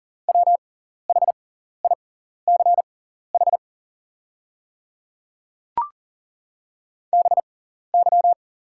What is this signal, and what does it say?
Morse code, heard as a keyed tone, about 40 wpm, spaced as for 14 wpm, 700 Hz. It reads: WHICH BY
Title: 6x Character Spacing - Top 100 Words - 40wpm